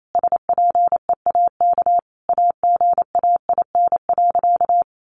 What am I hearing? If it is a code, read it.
Morse code, heard as a keyed tone, about 28 wpm, 700 Hz. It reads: SPEAK AGAIN.
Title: Sentences from Top 1500 Words 28wpm